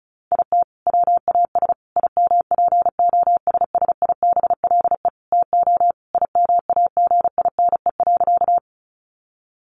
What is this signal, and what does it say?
Morse code, heard as a keyed tone, about 34 wpm, 700 Hz. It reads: IT WAS IMPOSSIBLE TO IMAGINE.
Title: Sentences from Top 1400 Words 35wpm